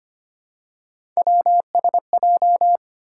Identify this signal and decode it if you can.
Morse code, heard as a keyed tone, about 25 wpm, 700 Hz. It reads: WSJ
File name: Three Letter-Number Combinations - 25wpm